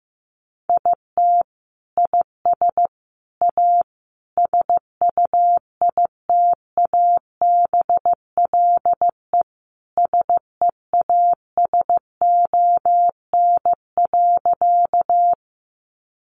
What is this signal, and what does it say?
Morse code, heard as a keyed tone, about 15 wpm, 700 Hz. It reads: IT IS A SUITABLE SEASON.
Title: Sentences from Top 1400 Words 15wpm